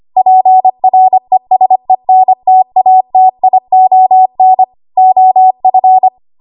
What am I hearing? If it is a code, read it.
Morse code, heard as a keyed tone, about 25 wpm, 750 Hz. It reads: PRESENTATION OF
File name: Archive / 25 WPM / 25WPM